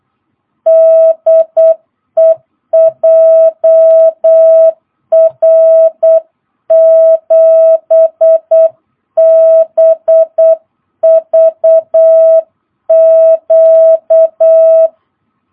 Morse code, heard as a keyed tone, about 8 wpm, 660 Hz. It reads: DE JR7BVQ